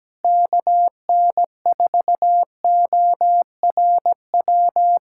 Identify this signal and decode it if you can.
Morse code, heard as a keyed tone, about 17 wpm, 700 Hz. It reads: KN4ORW
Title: USA Callsign DE Callsign - 6 Characters - 17wpm